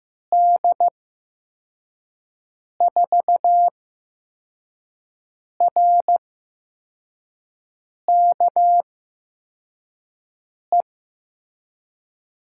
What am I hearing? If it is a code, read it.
Morse code, heard as a keyed tone, about 15 wpm, spaced as for 4 wpm, 700 Hz. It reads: D4RKE
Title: USA Call Signs - 6 Characters - 8x Character Spacing - 15wpm